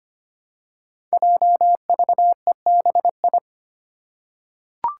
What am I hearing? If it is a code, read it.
Morse code, heard as a keyed tone, about 25 wpm, 700 Hz. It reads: JVEBI
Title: Five Challenging Letter-Number Combinations 25wpm